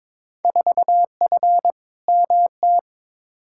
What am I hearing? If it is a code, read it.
Morse code, heard as a keyed tone, about 22 wpm, 700 Hz. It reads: 4F MT
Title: Field Day Exchanges Encore - 22wpm